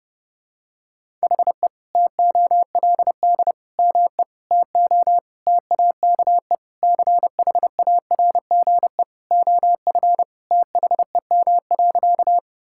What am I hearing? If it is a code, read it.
Morse code, heard as a keyed tone, about 30 wpm, 700 Hz. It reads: HE TOLD ME TO TAKE CHARGE OF THEM.
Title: Sentences from Top 1000 Words - Review Part 6 - 30wpm